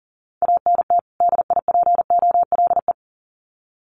Morse code, heard as a keyed tone, about 40 wpm, 700 Hz. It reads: ANT DIPOLE
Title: Common QSO Phrases 40wpm